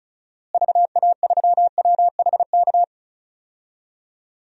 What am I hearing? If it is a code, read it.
Morse code, heard as a keyed tone, about 36 wpm, 700 Hz. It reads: VA3WHK
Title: DXCC-100 Call Signs - 6 Characters - Repeated 2x - 35wpm